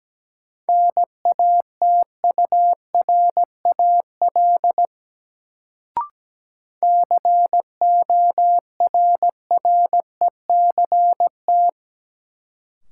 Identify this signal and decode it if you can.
Morse code, heard as a keyed tone, about 17 wpm, 700 Hz. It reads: NATURAL CORRECT
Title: Sentences from Top 700 Words - New Word Drill - 17wpm